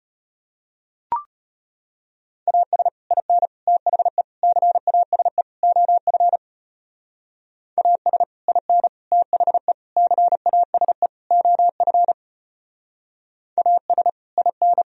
Morse code, heard as a keyed tone, about 32 wpm, 700 Hz. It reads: AS IN THE CASE OF AS IN THE CASE OF AS IN